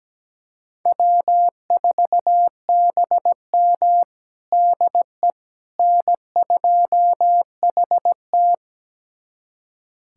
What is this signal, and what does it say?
Morse code, heard as a keyed tone, about 17 wpm, 700 Hz. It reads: W4BM DE N2HT
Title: USA Callsign DE Callsign - 4 Characters - 17wpm